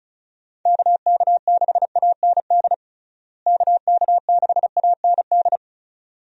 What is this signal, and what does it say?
Morse code, heard as a keyed tone, about 36 wpm, 700 Hz. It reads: KK6AND KK6AND
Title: USA Call Signs - 6 Characters - Repeated 2x - 35wpm